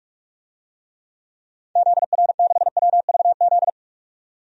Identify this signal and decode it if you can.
Morse code, heard as a keyed tone, about 44 wpm, 690 Hz. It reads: ZR6WVZ
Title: DXCC-100 Call Signs - 6 Characters - 45wpm